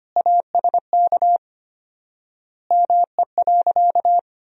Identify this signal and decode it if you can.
Morse code, heard as a keyed tone, about 25 wpm, 700 Hz. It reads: ASK ME.
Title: 4x Word-Spacing - Sentences from Top 500 Words - 25wpm